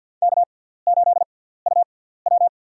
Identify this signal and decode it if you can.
Morse code, heard as a keyed tone, about 54 wpm, spaced as for 17 wpm, 690 Hz. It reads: K8UW
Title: USA Call Signs - 4 Characters - 6x Character Spacing - 50wpm